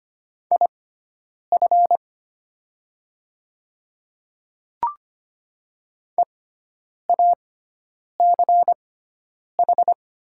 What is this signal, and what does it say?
Morse code, heard as a keyed tone, about 25 wpm, spaced as for 9 wpm, 690 Hz. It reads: IF EACH
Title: 6x Character Spacing - Top 100 Words - 25wpm